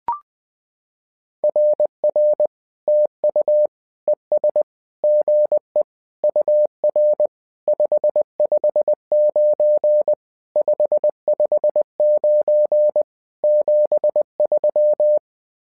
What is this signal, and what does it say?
Morse code, heard as a keyed tone, about 20 wpm, 600 Hz. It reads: RR TU ES GE UR 559 559 73
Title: SOTA QSO Practice 20wpm